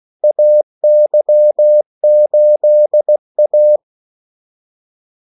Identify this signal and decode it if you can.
Morse code, heard as a keyed tone, about 16 wpm, 600 Hz. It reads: AY8A
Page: Kaldesignaler 16 wpm